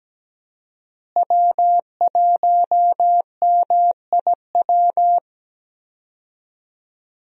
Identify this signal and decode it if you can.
Morse code, heard as a keyed tone, about 17 wpm, 700 Hz. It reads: W1MIW
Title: USA Call Signs - 5 Characters - Repeated 3x - 17wpm